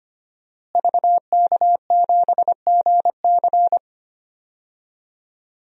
Morse code, heard as a keyed tone, about 25 wpm, 700 Hz. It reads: VK7GC
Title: DXCC-100 Call Signs - 5 Characters - Repeated 3x - 25wpm